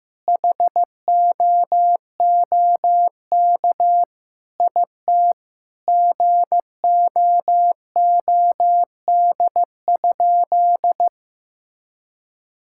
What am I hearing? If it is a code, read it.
Morse code, heard as a keyed tone, about 15 wpm, 700 Hz. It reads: HOOK IT GOOD?
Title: Sentences from Top 1600 Words 15wpm